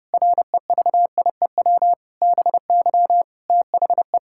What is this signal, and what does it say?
Morse code, heard as a keyed tone, about 31 wpm, 700 Hz. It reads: REVIEW BY THE
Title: Sentences from Top 1000 Words - Review Part 6 - 30wpm